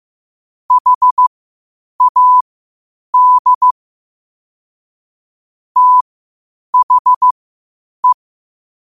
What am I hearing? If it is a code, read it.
Morse code, heard as a keyed tone, about 15 wpm, spaced as for 8 wpm, 1000 Hz. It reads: HAD THE
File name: qotdfiles / 8wpm